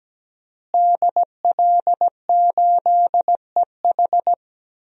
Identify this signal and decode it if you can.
Morse code, heard as a keyed tone, about 17 wpm, 700 Hz. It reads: DL8EH
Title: DXCC-100 Call Signs - 5 Characters - Repeated 2x - 17wpm